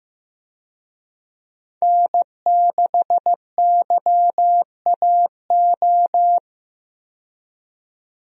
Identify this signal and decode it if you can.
Morse code, heard as a keyed tone, about 15 wpm, 700 Hz. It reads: N6YAO
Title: USA Call Signs - 5 Characters - 15wpm